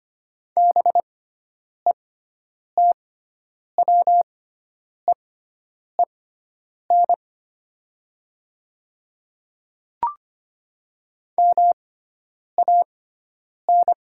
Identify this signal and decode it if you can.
Morse code, heard as a keyed tone, about 25 wpm, spaced as for 9 wpm, 700 Hz. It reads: BETWEEN MAN
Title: 6x Character Spacing - Top 500 Words - 25wpm